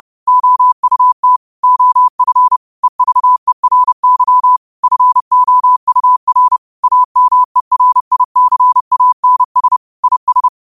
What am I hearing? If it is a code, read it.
Morse code, heard as a keyed tone, about 30 wpm, 1000 Hz. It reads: OUT OF EVERY FOUR AMERICANS IS